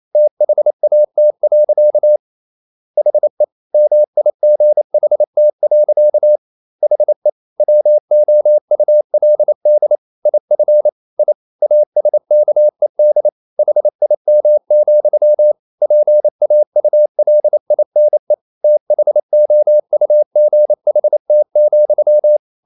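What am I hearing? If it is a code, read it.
Morse code, heard as a keyed tone, about 28 wpm, 600 Hz. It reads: THAT. HE MIGHT. HE WOULD IF I ASKED HIM, PAULINE THOUGHT,
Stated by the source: Morse code practice at 28 wpm, text of Ken Follett's novel Never in English, from CW med Gnister.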